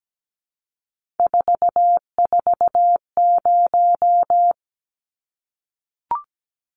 Morse code, heard as a keyed tone, about 17 wpm, 700 Hz. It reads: 440
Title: Numbers - 3-Digit 17wpm